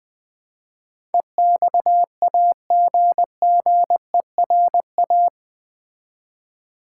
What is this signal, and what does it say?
Morse code, heard as a keyed tone, about 20 wpm, 700 Hz. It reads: EXAGGERA
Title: Build A Word - 4 Letters At A Time - 10-Letter Words - 20wpm